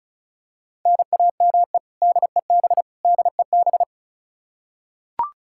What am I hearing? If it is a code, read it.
Morse code, heard as a keyed tone, about 34 wpm, 700 Hz. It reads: NAME DEB DEB
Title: Practice QSO - Exchange 1 - Beginning - 35wpm